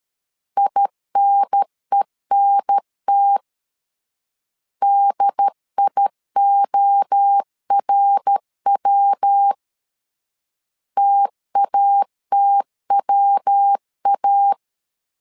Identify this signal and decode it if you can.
Morse code, heard as a keyed tone, about 13 wpm, 800 Hz. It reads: INENT DIORW TATWA
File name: LEZIONI / 3 MWRO / 3 MWRO 12wpm 1730